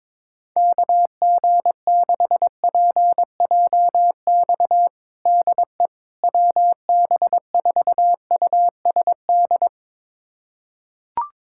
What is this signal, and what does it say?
Morse code, heard as a keyed tone, about 22 wpm, 700 Hz. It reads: KG6PJX DE WB4USD
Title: USA Callsign DE Callsign - 6 Characters - 22wpm